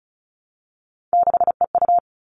Morse code, heard as a keyed tone, about 34 wpm, 700 Hz. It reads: 6EU
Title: All Cut Numbers - 3-Digit 35wpm